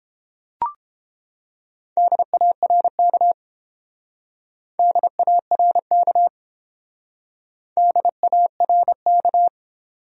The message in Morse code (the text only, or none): DARK DARK DARK